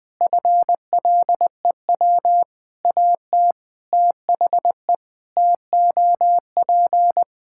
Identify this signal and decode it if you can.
Morse code, heard as a keyed tone, about 21 wpm, 700 Hz. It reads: FLEW AT THE TOP
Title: Sentences with 4-Word Phrases - Spoken Beginning - 20wpm